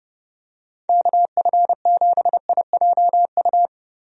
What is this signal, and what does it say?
Morse code, heard as a keyed tone, about 30 wpm, 700 Hz. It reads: KF7IJU
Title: USA Call Signs - 6 Characters - Repeated 3x - 30wpm